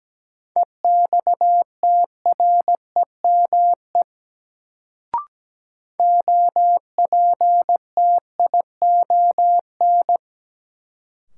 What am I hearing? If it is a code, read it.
Morse code, heard as a keyed tone, about 17 wpm, 700 Hz. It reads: EXTREME OPTION
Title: Sentences from Top 1300 Words - New Word Drill - 17wpm